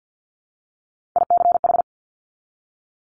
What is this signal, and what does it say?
Morse code, heard as a keyed tone, about 50 wpm, 700 Hz. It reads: ICH
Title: Top Three Letter Combinations in Words 50wpm